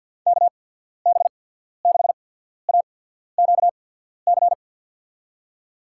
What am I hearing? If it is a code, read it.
Morse code, heard as a keyed tone, about 53 wpm, spaced as for 14 wpm, 700 Hz. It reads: KB6AQC
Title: USA Call Signs - 6 Characters - 8x Character Spacing - 50wpm